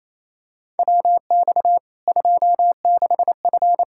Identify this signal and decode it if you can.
Morse code, heard as a keyed tone, about 28 wpm, 700 Hz. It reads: WX 26F
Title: Practice QSO - Exchange 2 - Info - 28wpm